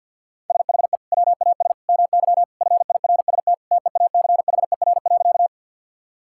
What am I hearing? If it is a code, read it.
Morse code, heard as a keyed tone, about 50 wpm, 700 Hz. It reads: SHE WAS MY FIRST TEACHER.
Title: Sentences from Top 1000 Words - Review Part 6 - 50wpm